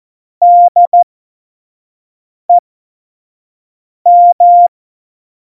Morse code, heard as a keyed tone, about 14 wpm, spaced as for 5 wpm, 700 Hz. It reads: DEM
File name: Morsekurs / Witze / Witz 23. Januar 2018, 25 F70